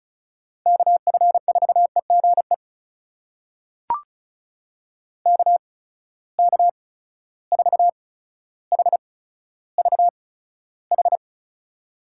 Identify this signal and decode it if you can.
Morse code, heard as a keyed tone, about 36 wpm, spaced as for 15 wpm, 700 Hz. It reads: KF4EGE KK4HVH
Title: USA Call Signs - 6 Characters - 8x Character Spacing - 35wpm